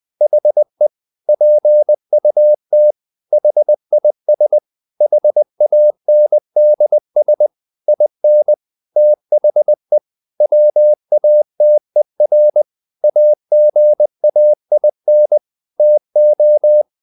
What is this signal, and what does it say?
Morse code, heard as a keyed tone, about 20 wpm, 600 Hz. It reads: HE PUT HIS HANDS IN THE WATER AGAIN TO
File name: the old man and the sea part 2 20wpm